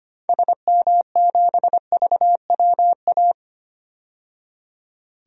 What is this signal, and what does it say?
Morse code, heard as a keyed tone, about 25 wpm, 690 Hz. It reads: SM7VWA
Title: DXCC-100 Call Signs - 6 Characters - 25wpm